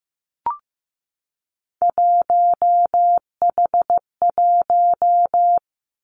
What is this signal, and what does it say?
Morse code, heard as a keyed tone, about 15 wpm, 700 Hz. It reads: 1H1
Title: Three Challenging Letter-Number Combinations 15wpm